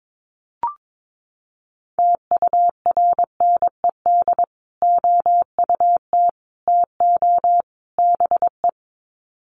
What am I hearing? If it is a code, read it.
Morse code, heard as a keyed tone, about 22 wpm, 700 Hz. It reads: TURNED OUT TO BE
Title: Sets of 4 Words 22wpm